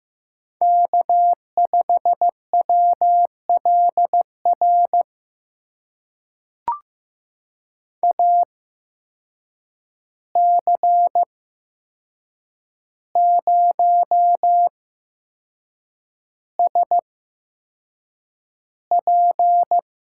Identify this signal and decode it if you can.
Morse code, heard as a keyed tone, about 15 wpm, spaced as for 6 wpm, 700 Hz. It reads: K5WLR AC0SP
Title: USA Call Signs - 5 Characters - 8x Character Spacing - 15wpm